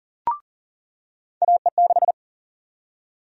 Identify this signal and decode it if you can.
Morse code, heard as a keyed tone, about 40 wpm, 700 Hz. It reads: AE6
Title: USA Call Sign Prefixes - 40wpm